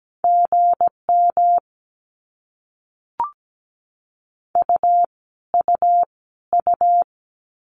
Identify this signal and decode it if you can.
Morse code, heard as a keyed tone, about 17 wpm, 700 Hz. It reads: GM U U U
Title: Abbreviations Repeated 3x - 17wpm